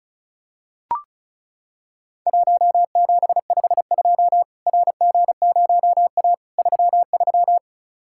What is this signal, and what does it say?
Morse code, heard as a keyed tone, about 36 wpm, 700 Hz. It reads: 1752 RG0A 33
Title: Worked All Europe DX Contest - QTC Exchange - 35wpm